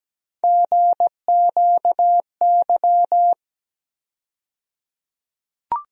Morse code, heard as a keyed tone, about 17 wpm, 700 Hz. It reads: GQY
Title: USA Call Sign Postfixes - 3 Letters 17wpm